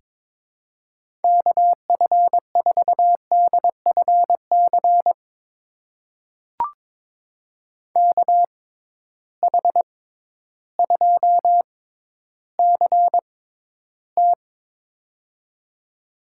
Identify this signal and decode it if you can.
Morse code, heard as a keyed tone, about 22 wpm, spaced as for 12 wpm, 700 Hz. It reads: KF4DFC KH2CT
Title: USA Call Signs - 6x Character Spacing - 22wpm